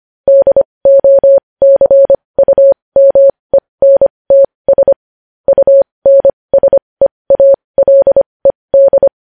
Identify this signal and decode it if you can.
Morse code, heard as a keyed tone, about 25 wpm, spaced as for 20 wpm, 550 Hz. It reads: DOCUMENTS UNSEALED
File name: podcast / News headlines-1x-25x20